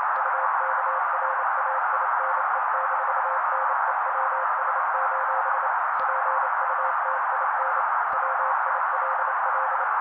Morse code, heard as a keyed tone, about 29 wpm, 530 Hz. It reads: UKRAINE-NEWS: PUTIN WILL
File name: daily / 27wpm noise